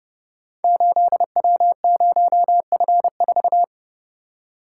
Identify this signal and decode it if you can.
Morse code, heard as a keyed tone, about 30 wpm, 700 Hz. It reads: 8W0F4
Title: Code Groups - 5 Letter:Numbers - 30wpm